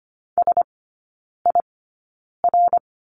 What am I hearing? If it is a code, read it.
Morse code, heard as a keyed tone, about 25 wpm, spaced as for 9 wpm, 700 Hz. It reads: SIR
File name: course / 25:10wpm / 093 Lesson 16e - Letter c - Three Characters 25wpm